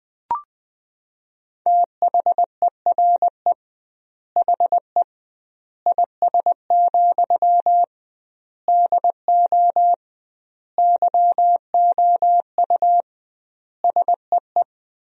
Morse code, heard as a keyed tone, about 20 wpm, 700 Hz. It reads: THERE HE IS, DO YOU SEE
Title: 2x Word-Spacing - Sentences from Top 100 Words - 20wpm